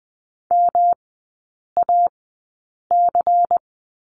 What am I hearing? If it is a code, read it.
Morse code, heard as a keyed tone, about 20 wpm, spaced as for 8 wpm, 700 Hz. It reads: MAC
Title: 166 Lesson 27e - Number 7 - Three Characters 20wpm